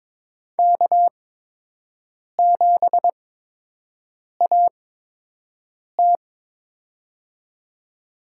Morse code, heard as a keyed tone, about 22 wpm, spaced as for 6 wpm, 700 Hz. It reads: K7AT
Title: USA Call Signs - 4 Characters - 8x Character Spacing - 22wpm